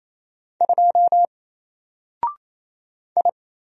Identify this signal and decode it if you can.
Morse code, heard as a keyed tone, about 27 wpm, 700 Hz. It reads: 2 I
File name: Single-Letter-Number-28wpm